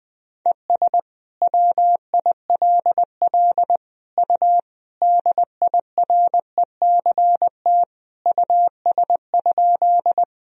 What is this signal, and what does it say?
Morse code, heard as a keyed tone, about 21 wpm, 700 Hz. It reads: ES WILL U DIRECT US?
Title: Sentences from Top 400 Words and Abbreviations 20wpm